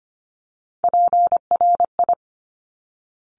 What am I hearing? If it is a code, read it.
Morse code, heard as a keyed tone, about 25 wpm, 700 Hz. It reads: PRI
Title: Top Three Letter Combinations in Words 25wpm